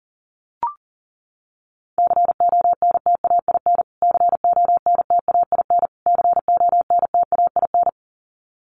Morse code, heard as a keyed tone, about 37 wpm, 700 Hz. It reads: CONTAIN CONTAIN CONTAIN
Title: Top 500 Words Repeated 40wpm